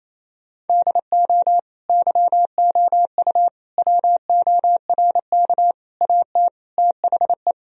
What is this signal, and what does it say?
Morse code, heard as a keyed tone, about 28 wpm, 700 Hz. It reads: DO YOU WORK AT THE
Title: Sentences from Top 1000 Words - Review Part 1 - 28wpm